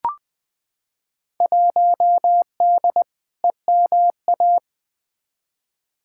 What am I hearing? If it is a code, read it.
Morse code, heard as a keyed tone, about 20 wpm, 700 Hz. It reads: 1D EMA
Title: Field Day Exchanges - 20wpm